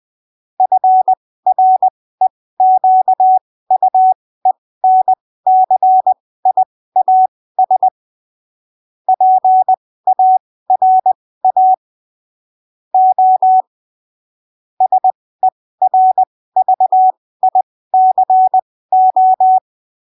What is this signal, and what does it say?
Morse code, heard as a keyed tone, about 20 wpm, spaced as for 15 wpm, 750 Hz. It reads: FREQUENCIAS PARA O SERVICO
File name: arquivos / texto / texto 07 20ppm